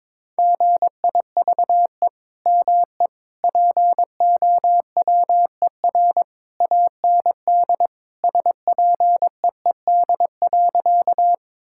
Morse code, heard as a keyed tone, about 22 wpm, 700 Hz. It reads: GIVE ME POWER AND SPEED.
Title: Sentences from Top 1000 Words - Review Part 5 - 22wpm